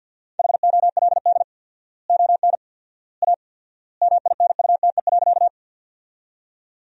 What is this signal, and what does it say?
Morse code, heard as a keyed tone, about 47 wpm, 700 Hz. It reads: HOLD ON A MINUTE.
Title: 4x Word-Spacing - Sentences from Top 500 Words - 50wpm